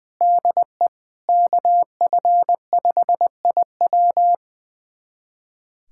Morse code, heard as a keyed tone, about 20 wpm, 700 Hz. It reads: DE KF5IW
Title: USA Callsign DE Callsign - 5 Characters - 20wpm